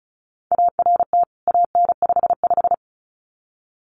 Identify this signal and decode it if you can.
Morse code, heard as a keyed tone, about 34 wpm, 700 Hz. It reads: ART AN55
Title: CWT Exchange 35wpm